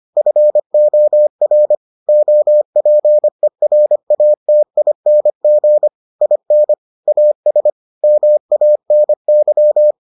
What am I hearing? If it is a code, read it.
Morse code, heard as a keyed tone, about 25 wpm, 600 Hz. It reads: FOR OPERATING IN AS MANY